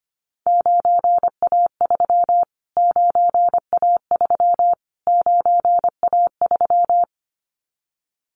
Morse code, heard as a keyed tone, about 25 wpm, 700 Hz. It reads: 9A3 9A3 9A3
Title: Callsign Prefixes for Top 25 DXCC - Repeated 3x - 25wpm